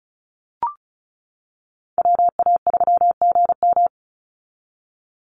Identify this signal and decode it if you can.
Morse code, heard as a keyed tone, about 34 wpm, 700 Hz. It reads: WA3GM
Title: Call Signs - 35wpm